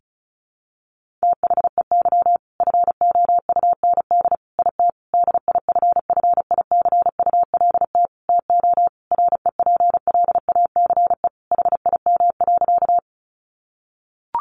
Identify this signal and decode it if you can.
Morse code, heard as a keyed tone, about 34 wpm, 700 Hz. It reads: THEY FOUND IT DIFFICULT TO REPLACE HIM.
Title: Sentences from Top 1400 Words 35wpm